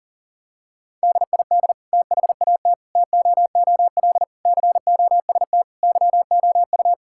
Morse code, heard as a keyed tone, about 40 wpm, 700 Hz. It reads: DID THAT TOOL COST YOU
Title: Sentences from Top 1000 Words - Review Part 6 - 40wpm